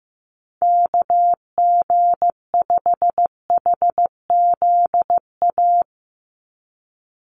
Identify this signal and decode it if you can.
Morse code, heard as a keyed tone, about 15 wpm, 700 Hz. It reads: KG5HZA
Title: USA Call Signs 15wpm